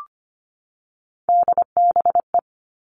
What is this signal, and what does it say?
Morse code, heard as a keyed tone, about 25 wpm, 700 Hz. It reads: DBE
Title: All Cut Numbers - 3-Digit 25wpm